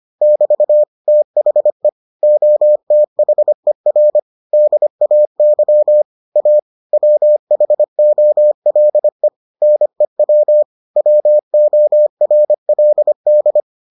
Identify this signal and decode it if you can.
Morse code, heard as a keyed tone, about 25 wpm, 600 Hz. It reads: <BT> THE OTHER DAY A WHOLE NEW WORLD